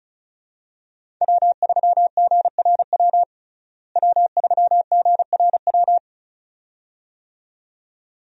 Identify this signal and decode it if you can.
Morse code, heard as a keyed tone, about 36 wpm, 700 Hz. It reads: W3GRW W3GRW
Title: USA Call Signs - 5 Characters - Repeated 2x - 35wpm